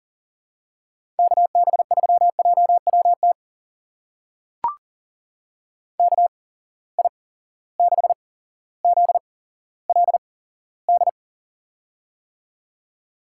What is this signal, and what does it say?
Morse code, heard as a keyed tone, about 39 wpm, spaced as for 17 wpm, 700 Hz. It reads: KB3JWT KI6ZLD
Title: USA Call Signs - 6 Characters - 8x Character Spacing - 40wpm